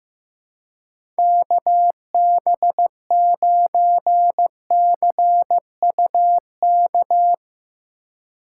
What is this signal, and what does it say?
Morse code, heard as a keyed tone, about 16 wpm, 700 Hz. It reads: KB9CUK
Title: USA Call Signs - 6 Characters - 2x Character Spacing - 15wpm